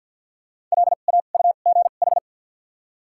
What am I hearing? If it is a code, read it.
Morse code, heard as a keyed tone, about 52 wpm, 700 Hz. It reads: L A U G H
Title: 2x Character Spacing - Top 500 Words - 50wpm